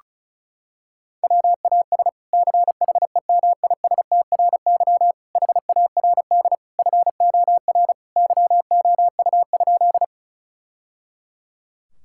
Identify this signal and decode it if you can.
Morse code, heard as a keyed tone, about 34 wpm, 700 Hz. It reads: WAS CHEMISTRY HARD FOR YOU?
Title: Sentences from Top 1000 Words - Review Part 6 - 35wpm